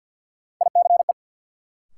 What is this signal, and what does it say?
Morse code, heard as a keyed tone, about 52 wpm, 700 Hz. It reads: ICE
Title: Sentences from Top 1000 Words - New Word Drill - 50wpm